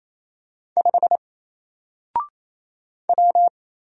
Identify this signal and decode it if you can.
Morse code, heard as a keyed tone, about 27 wpm, 700 Hz. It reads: 5 W
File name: Single-Letter-Number-28wpm